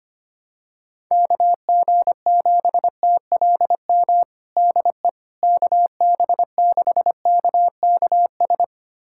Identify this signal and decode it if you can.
Morse code, heard as a keyed tone, about 25 wpm, 700 Hz. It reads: KG7TLM DE KB6KKS
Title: USA Callsign DE Callsign - 6 Characters - 25wpm